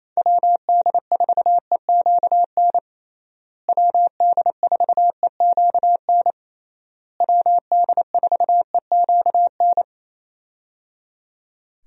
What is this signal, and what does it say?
Morse code, heard as a keyed tone, about 28 wpm, 700 Hz. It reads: WD4EQN WD4EQN WD4EQN
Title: USA Call Signs - 6 Characters - Repeated 3x - 28wpm